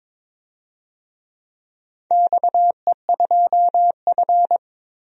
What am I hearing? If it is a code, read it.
Morse code, heard as a keyed tone, about 22 wpm, 700 Hz. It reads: XE2F
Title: DXCC-100 Call Signs - 4 Characters - 22wpm